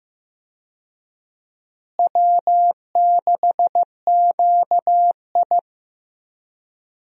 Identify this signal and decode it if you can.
Morse code, heard as a keyed tone, about 15 wpm, 700 Hz. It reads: W6QI
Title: USA Call Signs - 4 Characters - Repeated 3x - 15wpm